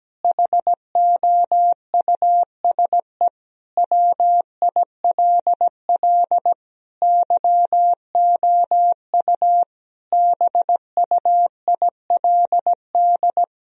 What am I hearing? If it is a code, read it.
Morse code, heard as a keyed tone, about 17 wpm, 700 Hz. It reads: HOUSE WILL YOU BUILD
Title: Sentences from Top 1000 Words - Review Part 1 - 17wpm